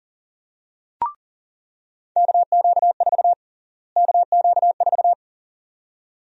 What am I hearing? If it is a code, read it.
Morse code, heard as a keyed tone, about 40 wpm, 700 Hz. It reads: KQ4 KQ4
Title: USA Call Sign Prefixes - Repeated 2x - 40wpm